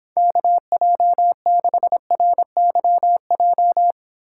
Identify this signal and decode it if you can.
Morse code, heard as a keyed tone, about 26 wpm, 700 Hz. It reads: KJ6RYJ